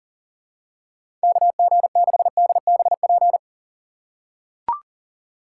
Morse code, heard as a keyed tone, about 39 wpm, 700 Hz. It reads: KG6DBP
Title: USA Call Signs - 6x Character Spacing - 40wpm